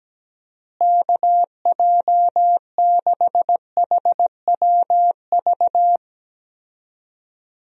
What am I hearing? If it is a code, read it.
Morse code, heard as a keyed tone, about 17 wpm, 700 Hz. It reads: KJ6HWV
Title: USA Call Signs - 6 Characters - Repeated 2x - 17wpm